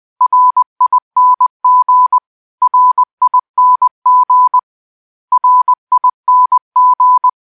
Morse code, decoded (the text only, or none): RING RING RING